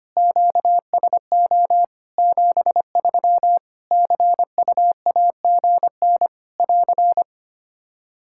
Morse code, heard as a keyed tone, about 25 wpm, 700 Hz. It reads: QSO 73 CUAGN <AR>
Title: Practice QSO - Exchange 4 - Ending - 25wpm